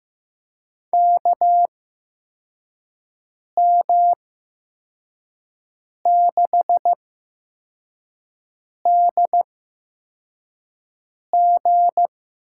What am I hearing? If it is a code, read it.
Morse code, heard as a keyed tone, about 15 wpm, spaced as for 4 wpm, 700 Hz. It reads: KM6DG
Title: USA Call Signs - 5 Characters - 8x Character Spacing - 15wpm